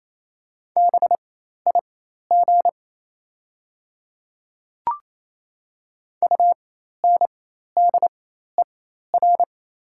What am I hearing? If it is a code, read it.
Morse code, heard as a keyed tone, about 28 wpm, spaced as for 13 wpm, 700 Hz. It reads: BIG UNDER
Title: 4x Character Spacing - Top 500 Words - 28wpm